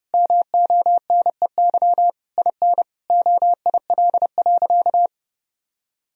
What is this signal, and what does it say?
Morse code, heard as a keyed tone, about 30 wpm, 700 Hz. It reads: MONEY IN OIL.
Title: Sentences from Top 1000 Words - Review Part 4 - 30wpm